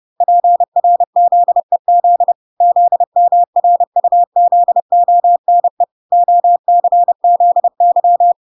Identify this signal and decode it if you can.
Morse code, heard as a keyed tone, about 30 wpm, 700 Hz. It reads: PRZEZ ZMRUZONE OCZY